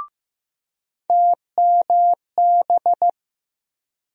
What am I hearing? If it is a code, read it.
Morse code, heard as a keyed tone, about 15 wpm, 700 Hz. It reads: TMB
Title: USA Call Sign Postfixes - 3 Letters 15wpm